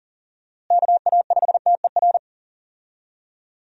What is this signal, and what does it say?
Morse code, heard as a keyed tone, about 40 wpm, 700 Hz. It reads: KA5TER
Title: USA Call Signs - 6 Characters - Repeated 3x - 40wpm